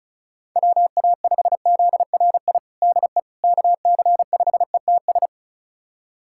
Morse code, heard as a keyed tone, about 34 wpm, 700 Hz. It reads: WA5ZRI DE KC5ETS
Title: USA Callsign DE Callsign 35wpm